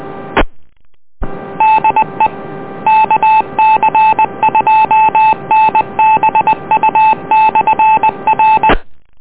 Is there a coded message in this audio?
Yes: Morse code from 0.4 to 8.7 s, heard as a keyed tone, about 20 wpm, 870 Hz, with noise. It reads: E DE KC2NBU/R